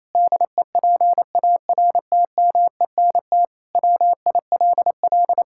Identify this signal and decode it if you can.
Morse code, heard as a keyed tone, about 28 wpm, 700 Hz. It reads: DEPARTMENT WILL
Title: Sentences from Top 1000 Words - Review Part 5 - 28wpm